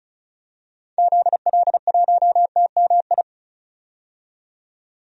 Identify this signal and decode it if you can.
Morse code, heard as a keyed tone, about 36 wpm, 700 Hz. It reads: ZL1TMI
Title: DXCC-100 Call Signs - 6 Characters - 35wpm